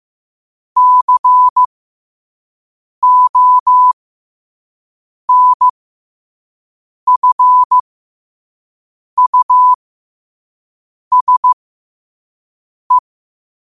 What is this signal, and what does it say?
Morse code, heard as a keyed tone, about 15 wpm, spaced as for 5 wpm, 1000 Hz. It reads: CONFUSE
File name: qotdfiles / 5wpm